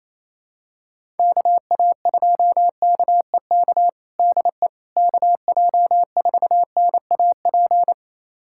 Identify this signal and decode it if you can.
Morse code, heard as a keyed tone, about 28 wpm, 700 Hz. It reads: KA2KEK DE KJ4NAP
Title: USA Callsign DE Callsign - 6 Characters - 28wpm